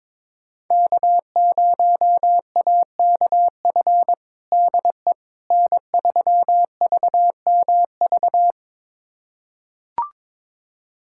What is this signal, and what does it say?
Morse code, heard as a keyed tone, about 22 wpm, 700 Hz. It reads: K0AKF DE N3VMV
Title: USA Callsign DE Callsign - 5 Characters - 22wpm